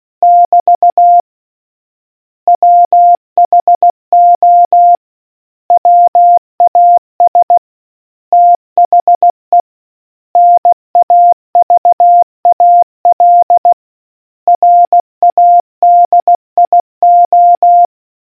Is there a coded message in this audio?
Yes: Morse code from 0.2 to 17.9 s, heard as a keyed tone, about 16 wpm, 700 Hz. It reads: <BT> WHO WAS THE NAVAL RADIO